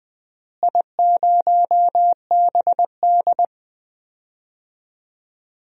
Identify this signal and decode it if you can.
Morse code, heard as a keyed tone, about 20 wpm, 700 Hz. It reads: I0BD
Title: DXCC-100 Call Signs - 4 Characters - Repeated 2x - 20wpm